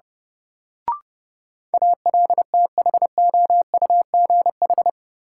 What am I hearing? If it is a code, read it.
Morse code, heard as a keyed tone, about 30 wpm, 700 Hz. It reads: ALTHOUGH
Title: Sentences from Top 600 Words - New Word Drill - 30wpm